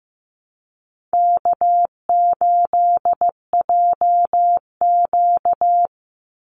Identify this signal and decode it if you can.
Morse code, heard as a keyed tone, about 15 wpm, 700 Hz. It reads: K8JQ
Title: Easy Call Signs Repeated 3x - 15wpm